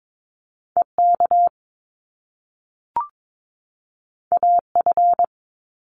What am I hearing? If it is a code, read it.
Morse code, heard as a keyed tone, about 22 wpm, 700 Hz. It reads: EK AF
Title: Top Two Letter Combinations in Words 22wpm